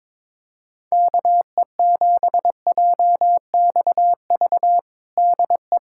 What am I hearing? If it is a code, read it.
Morse code, heard as a keyed tone, about 22 wpm, 700 Hz. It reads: KE7JXV DE
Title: USA Callsign DE Callsign 22wpm